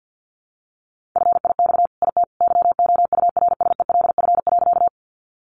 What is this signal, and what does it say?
Morse code, heard as a keyed tone, about 50 wpm, 700 Hz. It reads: FIX IT YOURSELF.
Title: Sentences from Top 1400 Words 50wpm